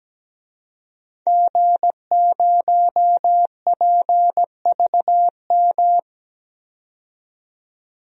Morse code, heard as a keyed tone, about 17 wpm, 700 Hz. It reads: G0PVM
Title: DXCC-100 Call Signs - 5 Characters - 17wpm